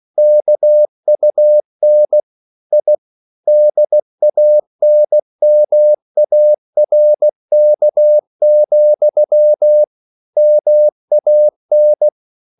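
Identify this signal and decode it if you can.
Morse code, heard as a keyed tone, about 16 wpm, 600 Hz. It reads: KUN I DANMARK, MAN
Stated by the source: Slow (QRS) Morse code at 16 wpm, from CW med Gnister.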